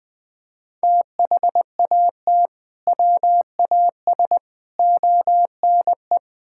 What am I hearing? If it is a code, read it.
Morse code, heard as a keyed tone, about 21 wpm, 700 Hz. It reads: THAT WAS ONE
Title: Build A Phrase - 5-Word Phrases - 20wpm